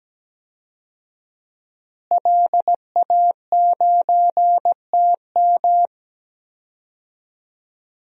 Morse code, heard as a keyed tone, about 17 wpm, 700 Hz. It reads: LA9TM
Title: DXCC-100 Call Signs - 5 Characters - Repeated 2x - 17wpm